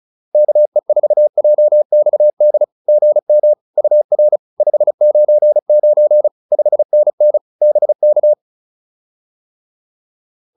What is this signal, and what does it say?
Morse code, heard as a keyed tone, about 34 wpm, 600 Hz. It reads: KE4JXD GM UR 599 5NN BK